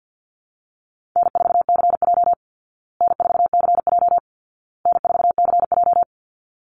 Morse code, heard as a keyed tone, about 50 wpm, 700 Hz. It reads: N4CJ N4CJ N4CJ
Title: Easy Call Signs Repeated 3x - 50wpm